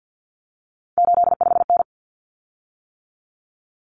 Morse code, heard as a keyed tone, about 50 wpm, 700 Hz. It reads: 85N
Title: Common Cut Numbers - 3-Digit 50wpm